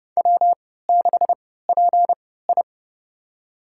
Morse code, heard as a keyed tone, about 32 wpm, spaced as for 17 wpm, 700 Hz. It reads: W6PI
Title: USA Call Signs - 4 Characters - 3x Character Spacing - 30wpm